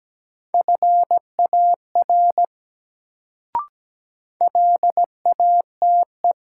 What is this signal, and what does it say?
Morse code, heard as a keyed tone, about 17 wpm, 700 Hz. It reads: FAR LATE
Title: Sentences from Top 300 Words - New Word Drill - 17wpm